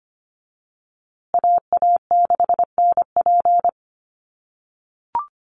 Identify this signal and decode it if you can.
Morse code, heard as a keyed tone, about 25 wpm, 700 Hz. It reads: AA6NP
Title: USA Call Signs Repeated 3x 25wpm